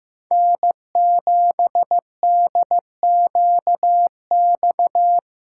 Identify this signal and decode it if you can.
Morse code, heard as a keyed tone, about 18 wpm, 700 Hz. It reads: N7DQX
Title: Speed Racing - USA 1x3 Call Signs 15wpm